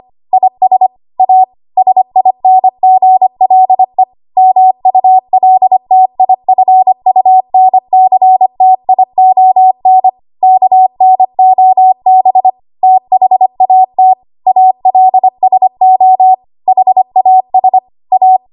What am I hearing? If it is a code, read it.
Morse code, heard as a keyed tone, about 25 wpm, 750 Hz. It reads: IS A SINGLE MULTIFUNCTION KNOB THAT ALSO HAS A
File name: Archive / 25 WPM / 25WPM